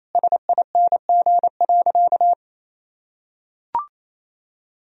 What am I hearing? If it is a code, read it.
Morse code, heard as a keyed tone, about 28 wpm, 700 Hz. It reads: SING.
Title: Sentences from Top 1000 Words - Review Part 2 - 28wpm